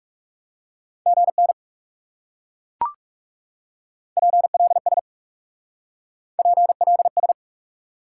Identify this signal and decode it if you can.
Morse code, heard as a keyed tone, about 38 wpm, 700 Hz. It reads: GN PLS PLS